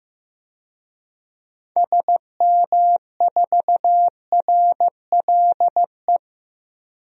Morse code, heard as a keyed tone, about 15 wpm, 700 Hz. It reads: SM4RLE
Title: DXCC-100 Call Signs - 6 Characters - 15wpm